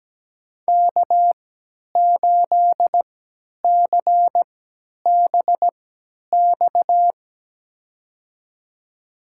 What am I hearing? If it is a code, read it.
Morse code, heard as a keyed tone, about 17 wpm, spaced as for 10 wpm, 700 Hz. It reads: K8CBX
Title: USA Call Signs - 5 Characters - 3x Character Spacing - 17wpm